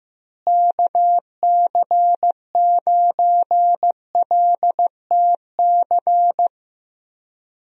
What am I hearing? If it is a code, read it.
Morse code, heard as a keyed tone, about 16 wpm, 700 Hz. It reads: KC9LTC
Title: USA Call Signs - 6 Characters - Repeated 2x - 15wpm